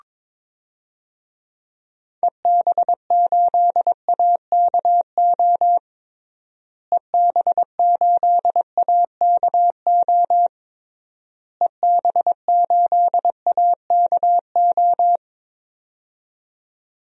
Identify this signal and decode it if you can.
Morse code, heard as a keyed tone, about 21 wpm, 700 Hz. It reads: EB8AKO EB8AKO EB8AKO
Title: DXCC-100 Call Signs - 6 Characters - Repeated 3x - 22wpm